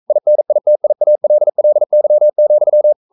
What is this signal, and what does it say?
Morse code, heard as a keyed tone, about 42 wpm, 600 Hz. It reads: INITIALLY,
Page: Romanen Never af Ken Follet på engelsk.